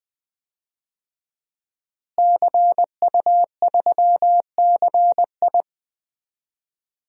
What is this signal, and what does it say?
Morse code, heard as a keyed tone, about 20 wpm, 700 Hz. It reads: CU3CI
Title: DXCC-100 Call Signs - 5 Characters - Repeated 2x - 20wpm